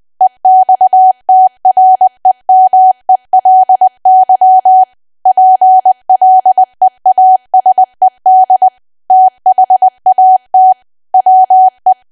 Morse code, heard as a keyed tone, about 20 wpm, 750 Hz. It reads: EXTREMELY PLEASED THAT WE
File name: Archive / 20 WPM / 20WPM